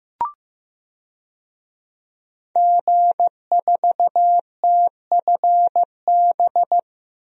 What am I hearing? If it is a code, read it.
Morse code, heard as a keyed tone, about 15 wpm, 700 Hz. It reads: G4TFB
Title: DXCC-100 Call Signs - 5 Characters - Repeated 3x - 15wpm